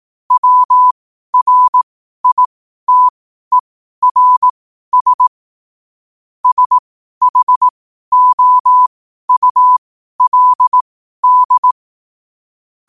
Morse code, heard as a keyed tone, about 18 wpm, spaced as for 12 wpm, 1000 Hz. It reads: WRITERS SHOULD
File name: qotdfiles / 12wpm